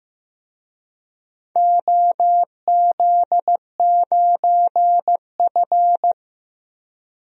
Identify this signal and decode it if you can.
Morse code, heard as a keyed tone, about 15 wpm, 700 Hz. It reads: OZ9F
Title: DXCC-100 Call Signs - 4 Characters - 15wpm